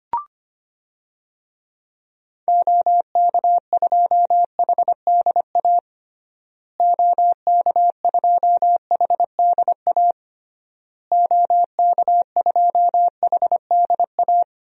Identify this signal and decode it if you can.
Morse code, heard as a keyed tone, about 25 wpm, 700 Hz. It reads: OK2HDA OK2HDA OK2HDA
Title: DXCC-100 Call Signs - 6 Characters - Repeated 3x - 25wpm